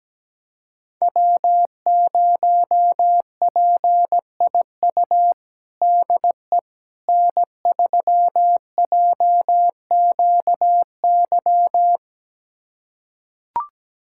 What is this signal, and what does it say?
Morse code, heard as a keyed tone, about 17 wpm, 700 Hz. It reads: W0PIU DE N3JQY
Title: USA Callsign DE Callsign - 5 Characters - 17wpm